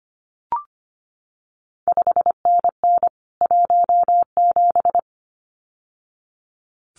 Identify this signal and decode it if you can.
Morse code, heard as a keyed tone, about 25 wpm, 700 Hz. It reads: 5NN 17
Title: CQ WW DX Contest Exchange Encore - 25wpm